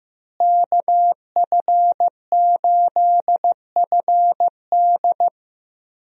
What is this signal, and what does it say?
Morse code, heard as a keyed tone, about 15 wpm, 700 Hz. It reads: KF8FD
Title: USA Call Signs - 5 Characters - Repeated 3x - 15wpm